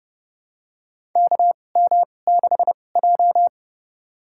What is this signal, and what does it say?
Morse code, heard as a keyed tone, about 30 wpm, spaced as for 22 wpm, 700 Hz. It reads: KM6J
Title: USA Call Signs - 4 Characters - 2x Character Spacing - 30wpm